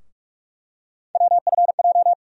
Morse code, heard as a keyed tone, about 43 wpm, 700 Hz. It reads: WFJ
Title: USA Call Sign Postfixes - 3 Letters 45wpm